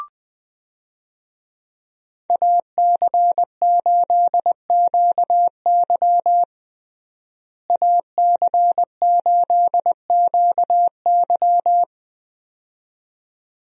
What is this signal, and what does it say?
Morse code, heard as a keyed tone, about 20 wpm, 700 Hz. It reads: AC8QY AC8QY
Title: USA Call Signs - 5 Characters - Repeated 2x - 20wpm